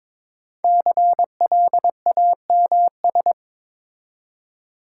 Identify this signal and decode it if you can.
Morse code, heard as a keyed tone, about 23 wpm, 690 Hz. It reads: CLAMS
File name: Difficult Words with Hints - 22wpm